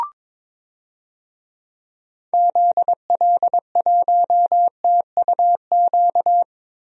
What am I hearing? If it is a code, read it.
Morse code, heard as a keyed tone, about 22 wpm, 700 Hz. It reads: ZL1TUQ
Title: DXCC-100 Call Signs - 6 Characters - 22wpm